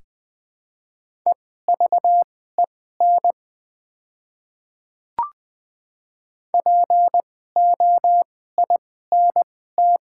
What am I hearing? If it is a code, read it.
Morse code, heard as a keyed tone, about 20 wpm, spaced as for 15 wpm, 700 Hz. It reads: EVEN POINT
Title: 2x Character Spacing - Top 300 Words - 20wpm